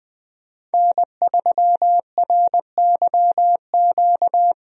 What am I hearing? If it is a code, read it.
Morse code, heard as a keyed tone, about 20 wpm, 700 Hz. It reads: N3RYQ
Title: USA Call Signs - 5 Characters - Repeated 2x - 20wpm